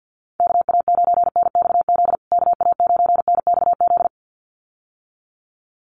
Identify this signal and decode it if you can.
Morse code, heard as a keyed tone, about 50 wpm, 700 Hz. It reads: KA9NXZ KA9NXZ
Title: USA Call Signs Repeated 2x 50wpm